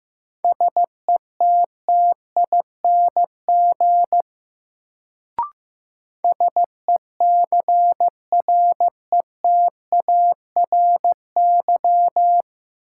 Morse code, heard as a keyed tone, about 15 wpm, 700 Hz. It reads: SETTING SECRETARY
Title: Sentences from Top 1300 Words - New Word Drill - 15wpm